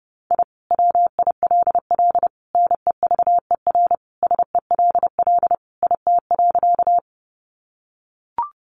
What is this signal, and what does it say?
Morse code, heard as a keyed tone, about 30 wpm, 700 Hz. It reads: I WILL NEVER SELL IT.
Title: Sentences from Top 1100 Words 30wpm